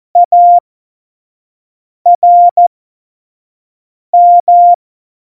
Morse code, heard as a keyed tone, about 14 wpm, spaced as for 5 wpm, 700 Hz. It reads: ARM